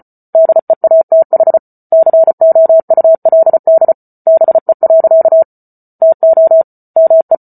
Morse code, heard as a keyed tone, about 34 wpm, 650 Hz. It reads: DEATH COULD BE. TO ME